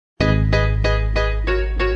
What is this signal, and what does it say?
There is no code.